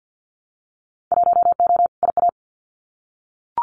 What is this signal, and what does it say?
Morse code, heard as a keyed tone, about 50 wpm, 700 Hz. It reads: 1O IA
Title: Winter Field Day Exchanges Encore 50wpm